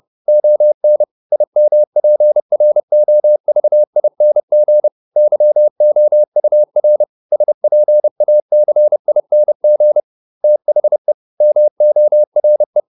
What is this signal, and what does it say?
Morse code, heard as a keyed tone, about 30 wpm, 600 Hz. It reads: ON IMPROVING YOUR SPACING THE MORE